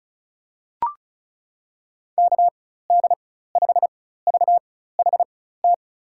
Morse code, heard as a keyed tone, about 36 wpm, spaced as for 16 wpm, 700 Hz. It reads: KD5VHT
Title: USA Call Signs - 6 Characters - 4x Character Spacing - 35wpm